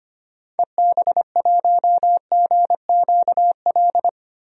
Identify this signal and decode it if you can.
Morse code, heard as a keyed tone, about 25 wpm, 700 Hz. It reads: EB1GQL
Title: DXCC-100 Call Signs - 6 Characters - 25wpm